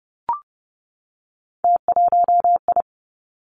Morse code, heard as a keyed tone, about 30 wpm, 700 Hz. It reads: T1I
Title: Three Challenging Letter-Number Combinations 30wpm